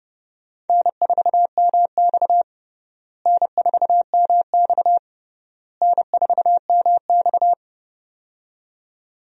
Morse code, heard as a keyed tone, about 30 wpm, 700 Hz. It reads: N4MX N4MX N4MX
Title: USA Call Signs - 4 Characters - Repeated 3x - 30wpm